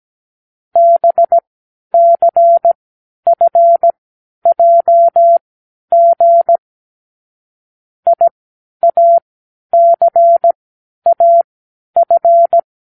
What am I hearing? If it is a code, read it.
Morse code, heard as a keyed tone, about 17 wpm, spaced as for 10 wpm, 680 Hz. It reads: BCFJG IACAF
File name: k7qo / Lesson18(A-L)